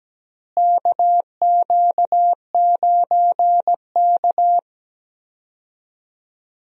Morse code, heard as a keyed tone, about 17 wpm, 700 Hz. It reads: KQ9K
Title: USA Call Signs - 4 Characters - Repeated 2x - 17wpm